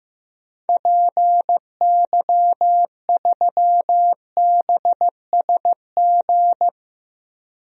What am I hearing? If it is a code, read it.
Morse code, heard as a keyed tone, about 15 wpm, 700 Hz. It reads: PY3BSG
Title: DXCC-100 Call Signs - 6 Characters - 15wpm